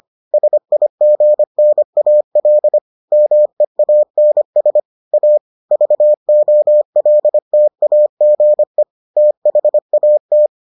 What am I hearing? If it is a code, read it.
Morse code, heard as a keyed tone, about 25 wpm, 600 Hz. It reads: SIGNAL MEANS A VOLTAGE THAT